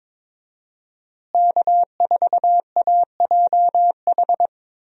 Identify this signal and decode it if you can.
Morse code, heard as a keyed tone, about 22 wpm, 700 Hz. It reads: K4AJH
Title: USA Call Signs - 5 Characters - Repeated 2x - 22wpm